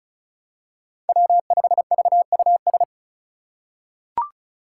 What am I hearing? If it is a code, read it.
Morse code, heard as a keyed tone, about 36 wpm, 700 Hz. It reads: W5VUS
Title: Five Challenging Letter-Number Combinations 35wpm